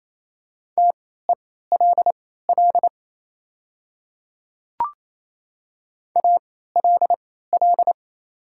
Morse code, heard as a keyed tone, about 27 wpm, spaced as for 16 wpm, 700 Hz. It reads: TELL ALL
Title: 3x Character Spacing - Top 500 Words - 28wpm